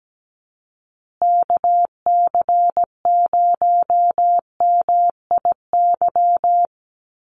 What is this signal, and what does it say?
Morse code, heard as a keyed tone, about 17 wpm, 700 Hz. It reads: KC0MIY
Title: USA Call Signs Repeated 3x 17wpm